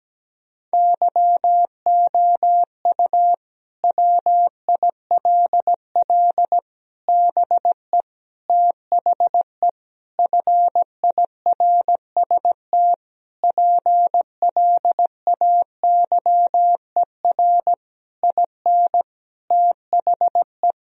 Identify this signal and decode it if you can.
Morse code, heard as a keyed tone, about 17 wpm, 700 Hz. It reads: YOU WILL BE THE FIRST PLAYER IN THE
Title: Sentences from Top 1000 Words - Review Part 4 - 17wpm